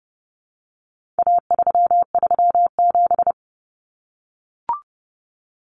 Morse code, heard as a keyed tone, about 30 wpm, 700 Hz. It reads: A337
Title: Common Cut Numbers - 4-Digit 30wpm